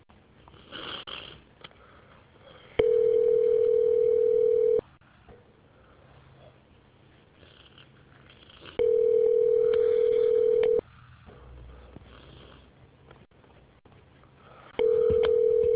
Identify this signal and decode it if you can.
Ringback tone (North America), 440 + 480 Hz.